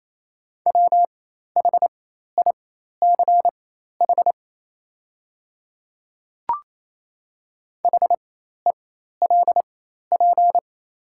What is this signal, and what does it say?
Morse code, heard as a keyed tone, about 28 wpm, spaced as for 13 wpm, 700 Hz. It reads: WHICH HELP
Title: 4x Character Spacing - Top 300 Words - 28wpm